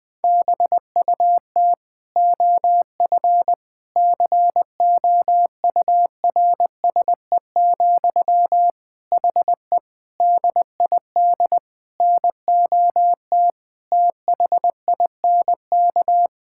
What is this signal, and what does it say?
Morse code, heard as a keyed tone, about 20 wpm, 700 Hz. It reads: BUT OF COURSE, HE DID NOT THINK
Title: Sentences from Top 1600 Words 20wpm